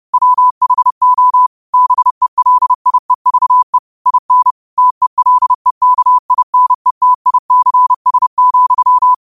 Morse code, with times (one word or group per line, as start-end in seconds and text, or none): 0.1-1.5 WHO
1.7-3.8 BELIEVE
4.1-4.5 IN
4.8-9.1 TELEKINETICS,